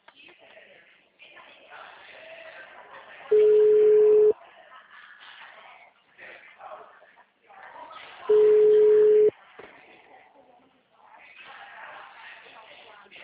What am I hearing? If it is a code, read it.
Ringback tone (Europe, 425 Hz).